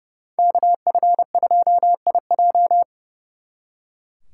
Morse code, heard as a keyed tone, about 36 wpm, 700 Hz. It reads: KF2IJ
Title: Speed Racing - USA 2x2 Call Signs 30wpm